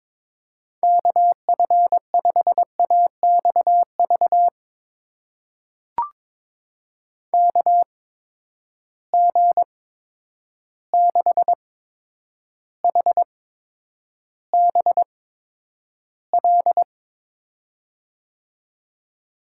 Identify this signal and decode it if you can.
Morse code, heard as a keyed tone, about 22 wpm, spaced as for 9 wpm, 700 Hz. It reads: KF5AXV KG6HBL
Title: USA Call Signs - 6 Characters - 8x Character Spacing - 22wpm